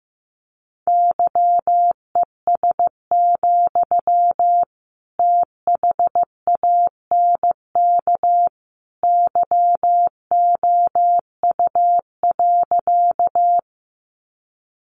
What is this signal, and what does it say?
Morse code, heard as a keyed tone, about 15 wpm, 700 Hz. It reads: YES, THANK YOU.
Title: Sentences from Top 600 Words 15wpm